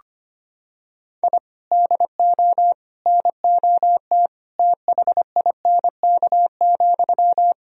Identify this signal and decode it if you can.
Morse code, heard as a keyed tone, about 25 wpm, 700 Hz. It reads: I DO NOT THINK,
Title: Sentences from Top 1000 Words - Review Part 3 - 25wpm